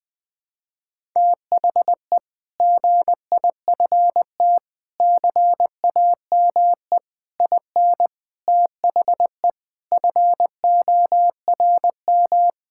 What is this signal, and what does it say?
Morse code, heard as a keyed tone, about 20 wpm, 700 Hz. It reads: THE GIFT CAME IN THE FORM
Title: Sentences with 5-Word Phrases - Spoken Beginning - 20wpm